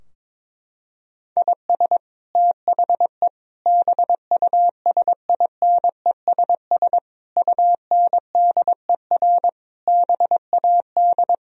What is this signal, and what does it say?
Morse code, heard as a keyed tone, about 22 wpm, 700 Hz. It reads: IS THE BUSINESS UNDER BAD
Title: Sentences from Top 1000 Words - Review Part 4 - 22wpm